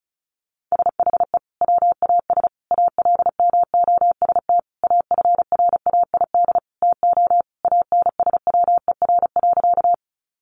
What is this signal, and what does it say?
Morse code, heard as a keyed tone, about 34 wpm, 700 Hz. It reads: SHE WAS ALMOST AFRAID TO ANSWER.
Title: Sentences from Top 1500 Words 35wpm